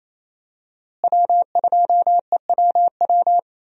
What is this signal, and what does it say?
Morse code, heard as a keyed tone, about 27 wpm, 700 Hz. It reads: W2EWW
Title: USA Call Signs - 5 Characters - Repeated 2x - 28wpm